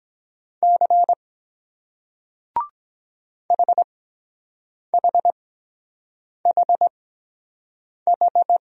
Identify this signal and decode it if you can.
Morse code, heard as a keyed tone, about 23 wpm, spaced as for 7 wpm, 700 Hz. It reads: C HHHH